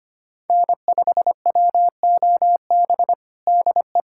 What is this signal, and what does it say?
Morse code, heard as a keyed tone, about 25 wpm, 700 Hz. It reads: N5WOB DE
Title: USA Callsign DE Callsign - 5 Characters - 25wpm